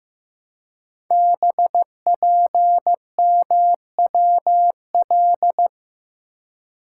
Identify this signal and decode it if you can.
Morse code, heard as a keyed tone, about 15 wpm, 700 Hz. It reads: BPMWL
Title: Code Groups - 5 Letters - 15wpm